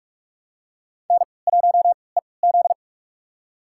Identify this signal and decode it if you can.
Morse code, heard as a keyed tone, about 41 wpm, spaced as for 25 wpm, 700 Hz. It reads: N1EZ
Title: USA Call Signs - 4 Characters - 3x Character Spacing - 45wpm